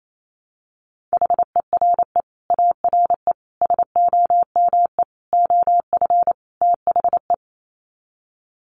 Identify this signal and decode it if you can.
Morse code, heard as a keyed tone, about 28 wpm, 700 Hz. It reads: HERE ARE SOME OF THE
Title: Sets of 5 Words Encore 28wpm